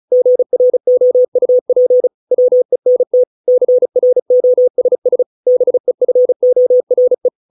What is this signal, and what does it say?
Morse code, heard as a keyed tone, about 34 wpm, 500 Hz. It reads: GROUP WENT CROSS BEFORE